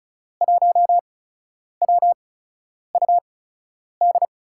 Morse code, heard as a keyed tone, about 36 wpm, spaced as for 10 wpm, 700 Hz. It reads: 1WUD
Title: USA Call Signs - 5 Characters - 8x Character Spacing - 35wpm